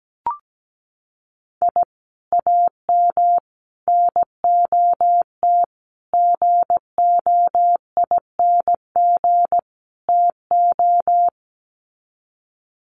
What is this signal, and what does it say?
Morse code, heard as a keyed tone, about 17 wpm, 690 Hz. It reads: I AM NOT GOING TO
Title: Sets of 5 Words Encore 17wpm